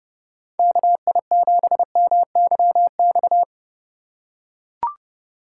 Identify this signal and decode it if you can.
Morse code, heard as a keyed tone, about 30 wpm, 700 Hz. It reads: KI7MYX
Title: USA Call Signs - 6 Characters - 2x Character Spacing - 30wpm